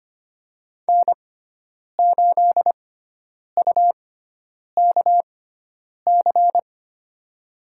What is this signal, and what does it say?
Morse code, heard as a keyed tone, about 25 wpm, spaced as for 9 wpm, 700 Hz. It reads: N8UKC
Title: USA Call Signs - 5 Characters - 6x Character Spacing - 25wpm